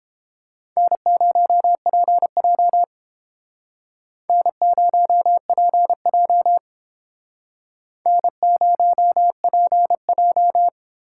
Morse code, heard as a keyed tone, about 32 wpm, 700 Hz. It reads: N0PJ N0PJ N0PJ